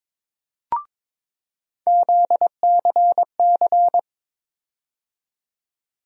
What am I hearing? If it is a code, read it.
Morse code, heard as a keyed tone, about 22 wpm, 700 Hz. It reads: ZCC
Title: USA Call Sign Postfixes - 3 Letters 22wpm